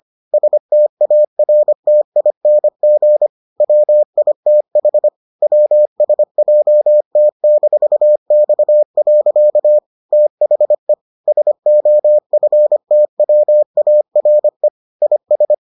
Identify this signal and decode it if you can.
Morse code, heard as a keyed tone, about 25 wpm, 600 Hz. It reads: STARTING WITH WSJT-X. THE SOFTWARE IS